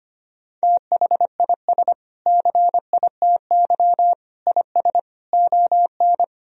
Morse code, heard as a keyed tone, about 25 wpm, 700 Hz. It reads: THIS CITY IS ON
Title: Sentences from Top 1000 Words - Review Part 4 - 25wpm